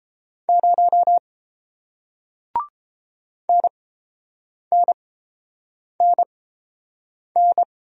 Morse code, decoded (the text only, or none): 0 NNNN